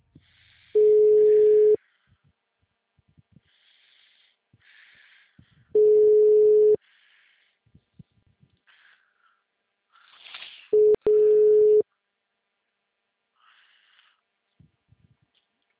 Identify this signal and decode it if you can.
Ringback tone (Europe, 425 Hz).
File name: out- -2007- - - .1147